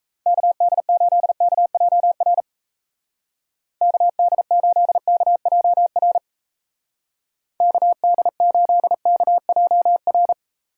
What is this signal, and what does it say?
Morse code, heard as a keyed tone, about 33 wpm, 700 Hz. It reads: KD8KJR KD8KJR KD8KJR
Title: Speed Racing - USA 2x3 Call Signs 28wpm